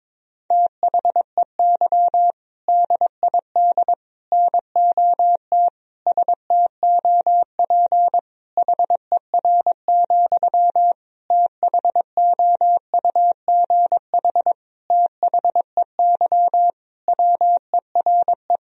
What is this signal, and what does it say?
Morse code, heard as a keyed tone, about 22 wpm, 700 Hz. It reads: THEY DID NOT STOP HER, THOUGH THEY WERE
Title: Sentences from Top 1000 Words - Review Part 2 - 22wpm